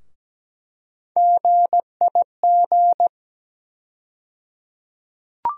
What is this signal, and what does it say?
Morse code, heard as a keyed tone, about 17 wpm, 700 Hz. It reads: GIG
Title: USA Call Sign Postfixes - 3 Letters 17wpm